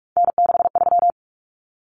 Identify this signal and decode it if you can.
Morse code, heard as a keyed tone, about 44 wpm, 700 Hz. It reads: N63
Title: Common Cut Numbers - 3-Digit 45wpm